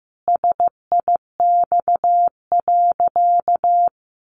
Morse code, heard as a keyed tone, about 14 wpm, 700 Hz. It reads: SIX.
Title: Sentences from Top 600 Words 15wpm